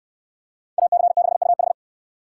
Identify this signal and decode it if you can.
Morse code, heard as a keyed tone, about 69 wpm, 700 Hz. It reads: AC6UH